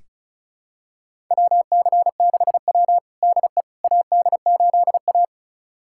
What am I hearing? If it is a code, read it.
Morse code, heard as a keyed tone, about 34 wpm, 690 Hz. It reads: WC6W DE AD8A
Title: USA Callsign DE Callsign - 4 Characters - 35wpm